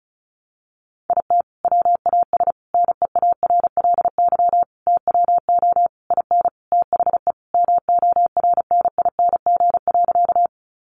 Morse code, heard as a keyed tone, about 34 wpm, 700 Hz. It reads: IT WAS NEARLY TWO IN THE MORNING.
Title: Sentences from Top 1300 Words 35wpm